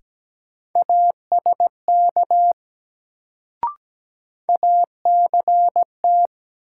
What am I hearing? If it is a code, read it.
Morse code, heard as a keyed tone, about 17 wpm, 700 Hz. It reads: ASK ACT
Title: Sentences from Top 200 Words - New Word Drill - 17wpm